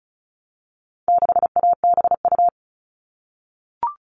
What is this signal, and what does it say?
Morse code, heard as a keyed tone, about 34 wpm, 700 Hz. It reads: 6ABU
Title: All Cut Numbers - 4-Digit 35wpm